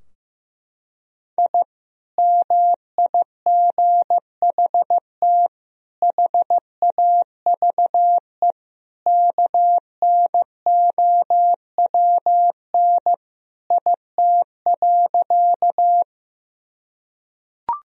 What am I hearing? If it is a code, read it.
Morse code, heard as a keyed tone, about 15 wpm, 700 Hz. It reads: I MIGHT HAVE KNOWN IT.
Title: Sentences from Top 1000 Words - Review Part 3 - 15wpm